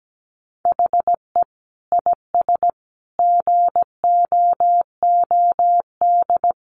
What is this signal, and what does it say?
Morse code, heard as a keyed tone, about 17 wpm, 700 Hz. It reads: HE IS GOOD
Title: Sentences from Top 1400 Words 17wpm